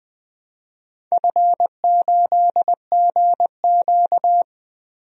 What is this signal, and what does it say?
Morse code, heard as a keyed tone, about 20 wpm, 700 Hz. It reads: F8GQ
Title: DXCC-100 Call Signs - 4 Characters - 20wpm